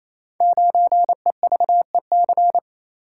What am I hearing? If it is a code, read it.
Morse code, heard as a keyed tone, about 27 wpm, 700 Hz. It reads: 9EVEC
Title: Code Groups - 5 Letter:Numbers - 28wpm